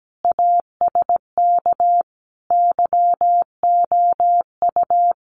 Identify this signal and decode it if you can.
Morse code, heard as a keyed tone, about 17 wpm, 700 Hz. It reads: ASK YOU
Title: Sets of 5 Words Encore2 17wpm